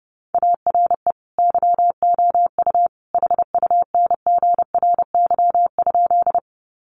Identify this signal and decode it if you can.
Morse code, heard as a keyed tone, about 30 wpm, 700 Hz. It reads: ARE YOU HUNGRY?
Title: Sentences from Top 1400 Words 30wpm